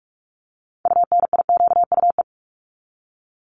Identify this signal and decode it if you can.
Morse code, heard as a keyed tone, about 44 wpm, 700 Hz. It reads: UNIQUE
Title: Top 751-1000 Words 45wpm